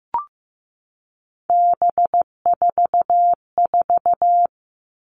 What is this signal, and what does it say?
Morse code, heard as a keyed tone, about 15 wpm, 700 Hz. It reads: B44
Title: All Cut Numbers - 3-Digit 15wpm